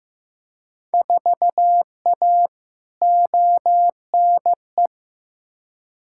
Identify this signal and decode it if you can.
Morse code, heard as a keyed tone, about 15 wpm, 700 Hz. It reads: 4A ONE
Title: Field Day Exchanges Encore - 15wpm